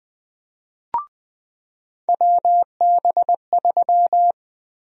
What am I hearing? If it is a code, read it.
Morse code, heard as a keyed tone, about 20 wpm, 700 Hz. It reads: WB3
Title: USA Call Sign Prefixes - 20wpm